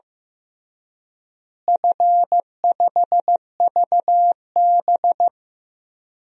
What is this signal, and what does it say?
Morse code, heard as a keyed tone, about 16 wpm, 700 Hz. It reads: F5VB
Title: DXCC-100 Call Signs - 4 Characters - Repeated 3x - 15wpm